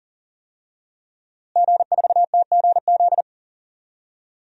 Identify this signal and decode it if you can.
Morse code, heard as a keyed tone, about 40 wpm, 700 Hz. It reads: G4TGZ
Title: DXCC-100 Call Signs - 5 Characters - Repeated 2x - 40wpm